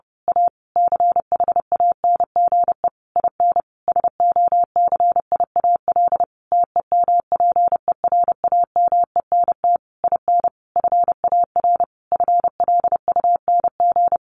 Morse code, heard as a keyed tone, about 30 wpm, 700 Hz. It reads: A CHANGE IN SOCIAL TEMPERAMENT IN FAR FLUNG
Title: A Sound Of Thunder 30wpm